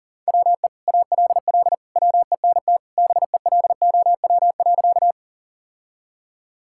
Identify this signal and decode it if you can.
Morse code, heard as a keyed tone, about 37 wpm, 700 Hz. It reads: WE ALL WENT BELOW.
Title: Sentences from Top 1000 Words - Review Part 5 - 40wpm